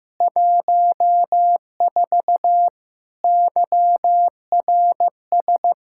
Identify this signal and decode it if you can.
Morse code, heard as a keyed tone, about 15 wpm, 700 Hz. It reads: 14 YRS
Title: K1USN SST QSO Practice 15wpm